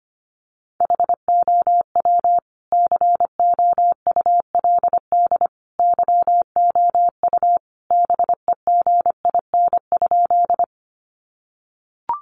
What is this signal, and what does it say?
Morse code, heard as a keyed tone, about 25 wpm, 700 Hz. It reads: HOW COULD YOU BEGIN?
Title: Sentences from Top 300 Words 25wpm